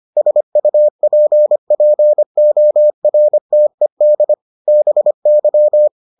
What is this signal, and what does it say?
Morse code, heard as a keyed tone, about 25 wpm, 600 Hz. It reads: SUPPORTED BY